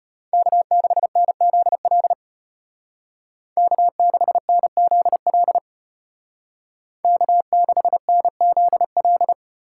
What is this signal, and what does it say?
Morse code, heard as a keyed tone, about 30 wpm, 700 Hz. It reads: K6NZL K6NZL K6NZL
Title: Speed Racing - USA 1x3 Call Signs 25wpm